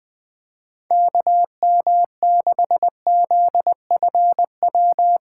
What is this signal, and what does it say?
Morse code, heard as a keyed tone, about 21 wpm, 700 Hz. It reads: KM6ZFW
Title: USA Call Signs - 6 Characters - 2x Character Spacing - 20wpm